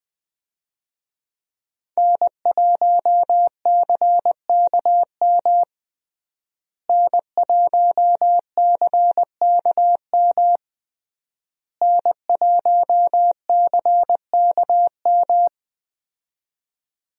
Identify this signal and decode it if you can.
Morse code, heard as a keyed tone, about 20 wpm, 700 Hz. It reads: N1CKM N1CKM N1CKM
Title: USA Call Signs - 5 Characters - Repeated 3x - 20wpm